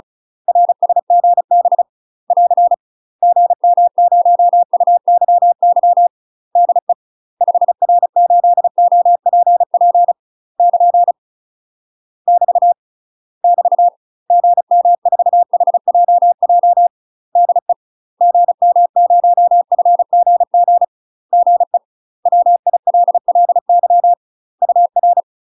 Morse code, heard as a keyed tone, about 35 wpm, 700 Hz. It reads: RSGB <AR> GM0UYY DE 5R8OPP <KN> <BT> <BT> GM4HJJ DE GM0FGG GE WILLY UR
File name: UK MASTER QSOs TRAINING / 35WPM / UK MASTER QSOs 0029